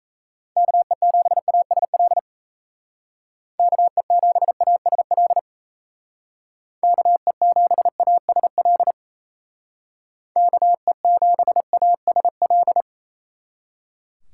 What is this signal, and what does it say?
Morse code, heard as a keyed tone, about 34 wpm, 700 Hz. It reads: KE7ASL KE7ASL KE7ASL KE7ASL
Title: Speed Racing - USA 2x3 Call Signs 28wpm